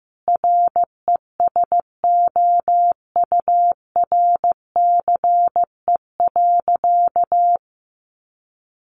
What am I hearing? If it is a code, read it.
Morse code, heard as a keyed tone, about 15 wpm, 700 Hz. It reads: RESOURCE.
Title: Sentences from Top 1300 Words 15wpm